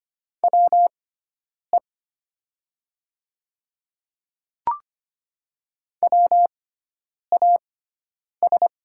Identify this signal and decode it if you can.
Morse code, heard as a keyed tone, about 25 wpm, spaced as for 9 wpm, 700 Hz. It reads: WE WAS
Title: 6x Character Spacing - Top 100 Words - 25wpm